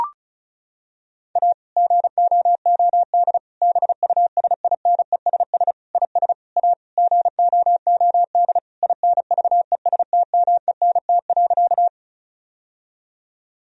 Morse code, heard as a keyed tone, about 36 wpm, 700 Hz. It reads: A GOOD BUSINESS IS A GOOD INVESTMENT.
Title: Sentences from Top 1000 Words - Review Part 4 - 35wpm